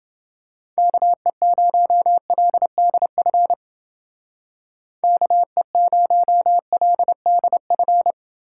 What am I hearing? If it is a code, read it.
Morse code, heard as a keyed tone, about 29 wpm, 700 Hz. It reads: KE0LDF KE0LDF